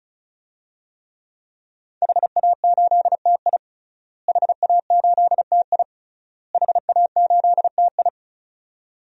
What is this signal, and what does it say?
Morse code, heard as a keyed tone, about 34 wpm, 700 Hz. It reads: HA8TI HA8TI HA8TI
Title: DXCC-100 Call Signs - 5 Characters - Repeated 3x - 35wpm